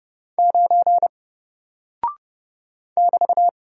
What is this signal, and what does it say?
Morse code, heard as a keyed tone, about 30 wpm, 700 Hz. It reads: 9 <BT>
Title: Single Character (Letter:Number:Punctuation:Prosign) 30wpm